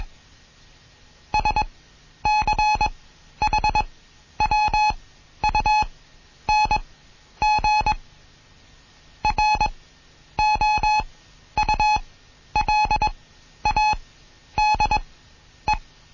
Morse code, heard as a keyed tone, about 21 wpm, spaced as for 11 wpm, 850 Hz. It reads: SCHWUNG ROULADE
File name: Morsen / Kurs A 20